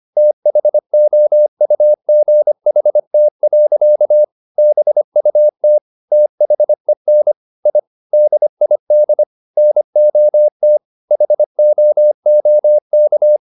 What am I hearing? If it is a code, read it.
Morse code, heard as a keyed tone, about 25 wpm, 600 Hz. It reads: THOUGHT. BUT THEN I DID NOT HOOK